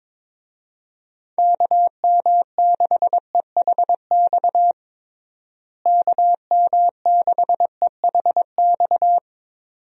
Morse code, heard as a keyed tone, about 22 wpm, 700 Hz. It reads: KM6EHX KM6EHX
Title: USA Call Signs - 6 Characters - Repeated 2x - 22wpm